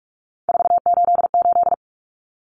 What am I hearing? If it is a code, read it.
Morse code, heard as a keyed tone, about 44 wpm, 700 Hz. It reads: 488
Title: Common Cut Numbers - 3-Digit 45wpm